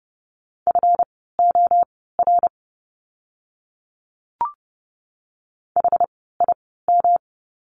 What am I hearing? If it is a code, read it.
Morse code, heard as a keyed tone, about 30 wpm, spaced as for 17 wpm, 700 Hz. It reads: FOR HIM
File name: course / 30:10wpm / 237 Lesson 38f - Letter q - Words 30wpm